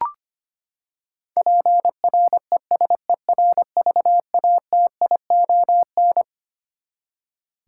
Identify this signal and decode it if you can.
Morse code, heard as a keyed tone, about 25 wpm, 700 Hz. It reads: PRESERVATION
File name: Twelve Letter Words - 25wpm